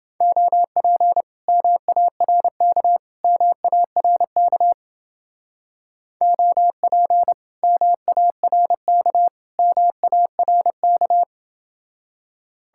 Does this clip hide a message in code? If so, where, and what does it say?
Yes: Morse code from 0.2 to 11.2 s, heard as a keyed tone, about 29 wpm, 700 Hz. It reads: OP MARK MARK OP MARK MARK